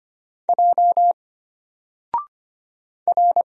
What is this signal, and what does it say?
Morse code, heard as a keyed tone, about 25 wpm, 700 Hz. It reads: J R
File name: Single-Letter-Number-25wpm